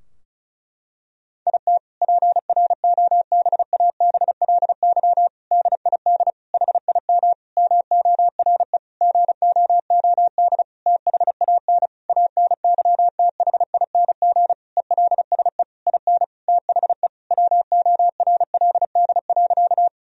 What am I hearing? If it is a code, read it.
Morse code, heard as a keyed tone, about 36 wpm, 700 Hz. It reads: IT PROBABLY DID HIM MORE GOOD THAN ANYTHING ELSE IN THE WORLD.
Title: Sentences from Top 1000 Words - Review Part 4 - 35wpm